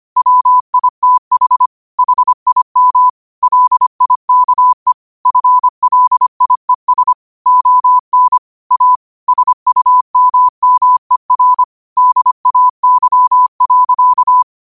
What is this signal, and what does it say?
Morse code, heard as a keyed tone, about 25 wpm, 1000 Hz. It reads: WITH HIM LIKE FLIES ON A SUMMER DAY.
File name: qotdfiles / 25wpm